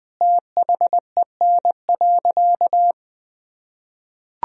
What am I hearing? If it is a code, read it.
Morse code, heard as a keyed tone, about 20 wpm, 700 Hz. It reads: THEN.
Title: Sentences from Top 200 Words and Abbreviations 20wpm